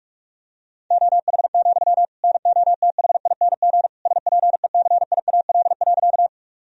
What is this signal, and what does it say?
Morse code, heard as a keyed tone, about 41 wpm, 700 Hz. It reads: OH, NOTHING SPECIAL.
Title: Sentences from Top 1000 Words - Review Part 3 - 45wpm